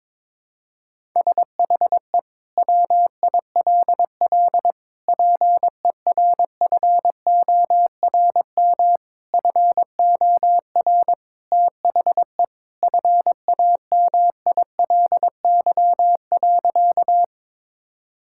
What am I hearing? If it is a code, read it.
Morse code, heard as a keyed tone, about 22 wpm, 700 Hz. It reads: SHE WILL PERFORM FOR THE FAMILY.
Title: Sentences from Top 1000 Words - Review Part 5 - 22wpm